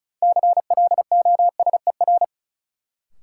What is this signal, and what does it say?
Morse code, heard as a keyed tone, about 36 wpm, 700 Hz. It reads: CLOSER
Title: Sets of 5 Words Encore - New Word Drill - 35wpm